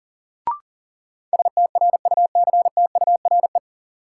Morse code, heard as a keyed tone, about 40 wpm, 700 Hz. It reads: STRUCTURE
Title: Sentences from Top 1000 Words - Review Part 6 - New Word Drill - 40wpm